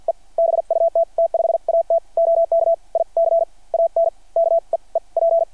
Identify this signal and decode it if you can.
Morse code, heard as a keyed tone, about 50 wpm, spaced as for 38 wpm, 690 Hz. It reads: E BUT T5AT OK I CAN KEEP
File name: archived / CRSnet